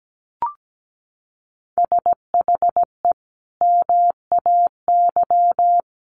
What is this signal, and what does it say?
Morse code, heard as a keyed tone, about 17 wpm, 700 Hz. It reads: SHE MAY
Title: Sentences from Top 1500 Words 17wpm